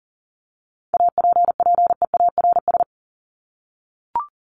Morse code, heard as a keyed tone, about 40 wpm, 700 Hz. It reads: APPEARS
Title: ES:S Suffix - Top 100 Words with Base Words 40wpm